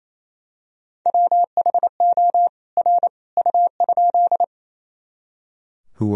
Morse code, heard as a keyed tone, about 28 wpm, 700 Hz. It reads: WHO R U?
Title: Sentences from Top 100 Words and Abbreviations 28wpm